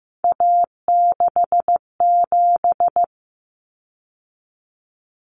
Morse code, heard as a keyed tone, about 15 wpm, 700 Hz. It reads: A67
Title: Common Cut Numbers - 3-Digit 15wpm